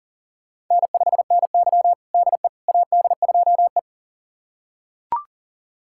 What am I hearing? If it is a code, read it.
Morse code, heard as a keyed tone, about 40 wpm, 700 Hz. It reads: N5NY DE AD2E
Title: USA Callsign DE Callsign - 4 Characters - 40wpm